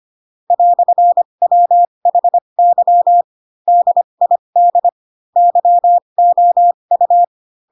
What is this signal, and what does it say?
Morse code, heard as a keyed tone, about 25 wpm, spaced as for 22 wpm, 700 Hz. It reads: "WHY DID YOU